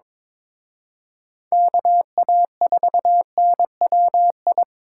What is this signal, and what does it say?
Morse code, heard as a keyed tone, about 25 wpm, 700 Hz. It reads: KA4NWI
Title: Speed Racing - USA 2x3 Call Signs 22wpm